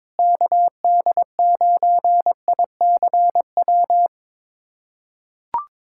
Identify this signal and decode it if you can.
Morse code, heard as a keyed tone, about 22 wpm, 700 Hz. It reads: KD9ICW
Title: USA Callsign DE Callsign 22wpm